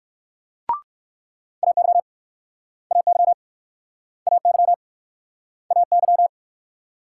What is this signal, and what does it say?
Morse code, heard as a keyed tone, about 61 wpm, 700 Hz. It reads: AY AY AY AY